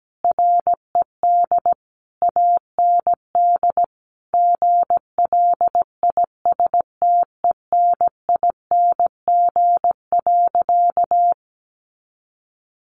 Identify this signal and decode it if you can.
Morse code, heard as a keyed tone, about 17 wpm, 700 Hz. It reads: RED AND GLISTENING.
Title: A Sound Of Thunder 17wpm